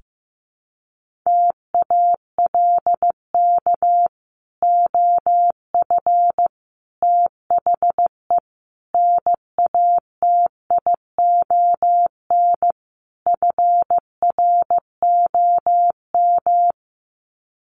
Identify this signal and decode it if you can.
Morse code, heard as a keyed tone, about 15 wpm, 700 Hz. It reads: TALK OF THE NATION FROM
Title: Sets of 5 Words 15wpm